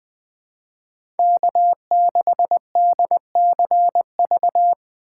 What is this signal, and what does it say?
Morse code, heard as a keyed tone, about 20 wpm, 700 Hz. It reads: K6DCV
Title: USA Call Signs - 5 Characters - 2x Character Spacing - 20wpm